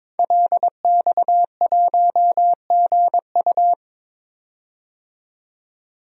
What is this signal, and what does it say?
Morse code, heard as a keyed tone, about 22 wpm, 700 Hz. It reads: LX1GU
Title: DXCC-100 Call Signs - 5 Characters - 22wpm